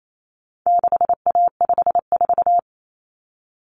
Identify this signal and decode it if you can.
Morse code, heard as a keyed tone, about 28 wpm, 700 Hz. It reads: 6A54
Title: Common Cut Numbers - 4-Digit 28wpm